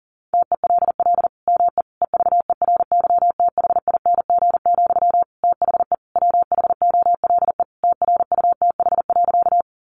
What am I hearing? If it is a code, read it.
Morse code, heard as a keyed tone, about 40 wpm, 700 Hz. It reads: TELL ME EVERYTHING, THE WHOLE TRUTH.
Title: Sentences from Top 800 Words 40wpm